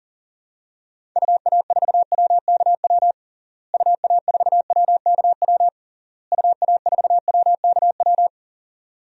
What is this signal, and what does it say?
Morse code, heard as a keyed tone, about 40 wpm, 700 Hz. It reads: UA4WKW UA4WKW UA4WKW
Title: DXCC-100 Call Signs - 6 Characters - Repeated 3x - 40wpm